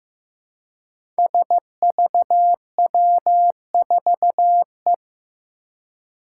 Morse code, heard as a keyed tone, about 15 wpm, 700 Hz. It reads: SVW4E
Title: Five Challenging Letter-Number Combinations 15wpm